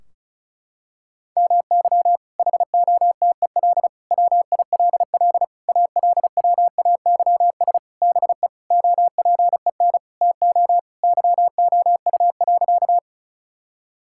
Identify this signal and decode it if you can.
Morse code, heard as a keyed tone, about 36 wpm, 700 Hz. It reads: MY HOTEL WILL ALWAYS BE OPEN TO YOU.
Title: Sentences from Top 1600 Words 35wpm